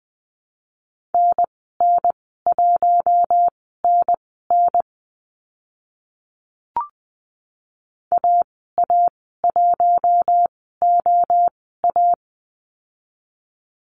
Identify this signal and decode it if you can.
Morse code, heard as a keyed tone, about 20 wpm, spaced as for 14 wpm, 700 Hz. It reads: NN1NN AA1OA
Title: 042 Lesson 8g - Number 1 - Call Signs 20wpm